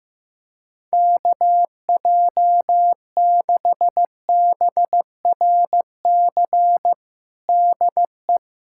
Morse code, heard as a keyed tone, about 16 wpm, 700 Hz. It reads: KJ6BRC DE
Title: USA Callsign DE Callsign - 6 Characters - 15wpm